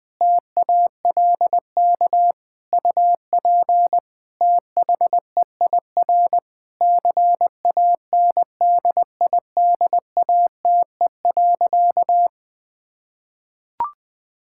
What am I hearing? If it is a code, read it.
Morse code, heard as a keyed tone, about 20 wpm, 700 Hz. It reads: TALK UP THEIR CANDIDATE.
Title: Sentences from Top 1600 Words 20wpm